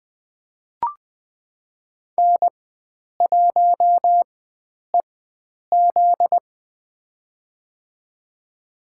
Morse code, heard as a keyed tone, about 20 wpm, spaced as for 9 wpm, 700 Hz. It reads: N1EZ
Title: USA Call Signs - 4x Character Spacing - 20wpm